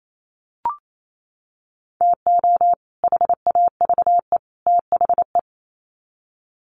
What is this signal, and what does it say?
Morse code, heard as a keyed tone, about 28 wpm, 700 Hz. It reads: TO HAVE THE
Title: Sets of 3 Words 28wpm